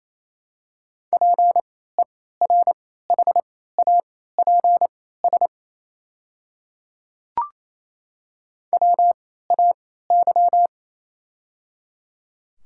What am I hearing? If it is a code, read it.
Morse code, heard as a keyed tone, about 28 wpm, spaced as for 16 wpm, 700 Hz. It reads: PERHAPS WAY
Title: 3x Character Spacing - Top 500 Words - 28wpm